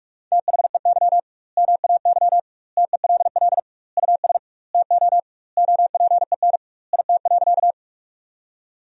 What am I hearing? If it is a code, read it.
Morse code, heard as a keyed tone, about 43 wpm, 700 Hz. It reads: THEY MAY TELL US TO OPEN IT.
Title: 2x Word-Spacing - Sentences from Top 300 Words - 45wpm